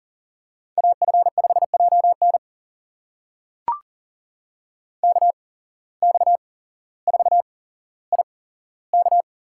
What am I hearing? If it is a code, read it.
Morse code, heard as a keyed tone, about 40 wpm, spaced as for 17 wpm, 700 Hz. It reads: AF5JN KX4IK
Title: USA Call Signs - 5 Characters - 8x Character Spacing - 40wpm